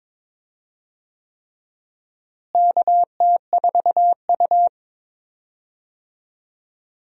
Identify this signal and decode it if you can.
Morse code, heard as a keyed tone, about 22 wpm, 700 Hz. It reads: KT4U
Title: USA Call Signs - 4 Characters - Repeated 3x - 22wpm